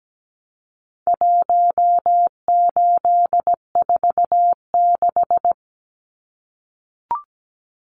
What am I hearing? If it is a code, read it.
Morse code, heard as a keyed tone, about 17 wpm, 700 Hz. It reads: 1846
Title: Numbers - Contest Style - 4-Digit 17wpm